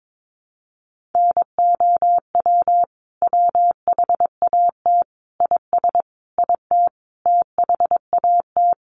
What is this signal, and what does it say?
Morse code, heard as a keyed tone, about 22 wpm, 700 Hz. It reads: NOW WHAT IS IT THAT
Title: Sentences from Top 200 Words 22wpm